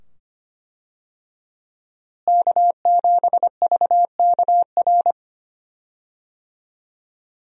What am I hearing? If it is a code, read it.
Morse code, heard as a keyed tone, about 25 wpm, 700 Hz. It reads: K7VKR
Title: USA Call Signs - 5 Characters - Repeated 2x - 25wpm